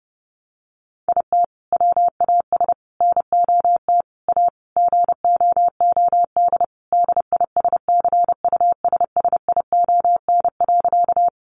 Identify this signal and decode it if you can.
Morse code, heard as a keyed tone, about 30 wpm, 700 Hz. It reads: IT WAS NOT A GOOD DISCUSSION.
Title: Sentences from Top 1100 Words 30wpm